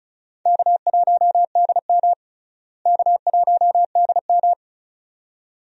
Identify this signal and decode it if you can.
Morse code, heard as a keyed tone, about 36 wpm, 700 Hz. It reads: K1DM K1DM
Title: USA Call Signs - 4 Characters - Repeated 2x - 35wpm